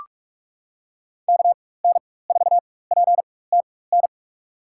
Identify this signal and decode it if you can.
Morse code, heard as a keyed tone, about 47 wpm, spaced as for 21 wpm, 700 Hz. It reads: KN4PTN
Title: USA Call Signs - 4x Character Spacing - 45wpm